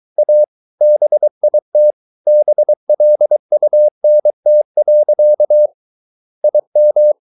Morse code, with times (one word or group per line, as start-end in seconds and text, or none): 0.2-0.4 A
0.8-1.9 BIT
2.3-5.7 BLUNT.
6.4-7.1 IM